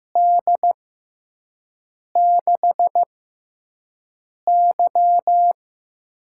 Morse code, heard as a keyed tone, about 15 wpm, spaced as for 5 wpm, 700 Hz. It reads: D6Y
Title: USA Call Signs - 6x Character Spacing - 15wpm